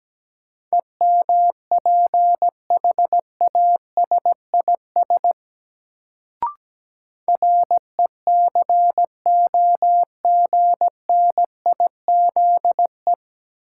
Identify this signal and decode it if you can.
Morse code, heard as a keyed tone, about 17 wpm, 700 Hz. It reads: EMPHASIS RECOGNIZE
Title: Sentences from Top 1400 Words - New Word Drill Alt - 17wpm